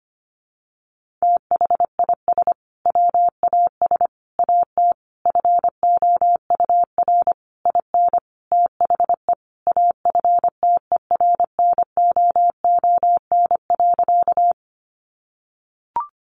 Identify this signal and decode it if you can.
Morse code, heard as a keyed tone, about 25 wpm, 700 Hz. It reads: THIS WAS AT FOUR IN THE AFTERNOON.
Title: Sentences from Top 1500 Words 25wpm